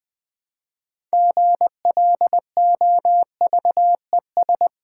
Morse code, heard as a keyed tone, about 20 wpm, 700 Hz. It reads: GLOVES
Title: Difficult Words with Hints - 20wpm